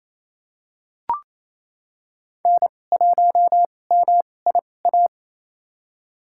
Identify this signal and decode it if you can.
Morse code, heard as a keyed tone, about 28 wpm, spaced as for 20 wpm, 700 Hz. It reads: N1MIA
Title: USA Call Signs - 5 Characters - 2x Character Spacing - 28wpm